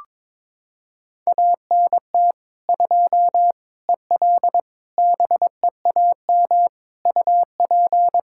Morse code, heard as a keyed tone, about 22 wpm, 700 Hz. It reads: ANT 2 EL BEAM UP
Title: Practice QSO - Exchange 2 - Info - 22wpm